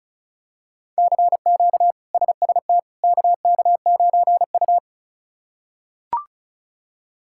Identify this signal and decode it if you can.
Morse code, heard as a keyed tone, about 36 wpm, 700 Hz. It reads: CQ SST KK9U
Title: K1USN SST QSO Practice - 35wpm